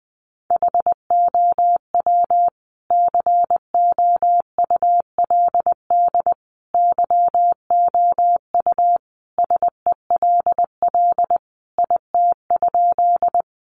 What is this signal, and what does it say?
Morse code, heard as a keyed tone, about 20 wpm, 700 Hz. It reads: HOW COULD YOU SELL IT?
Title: Sentences from Top 1100 Words 20wpm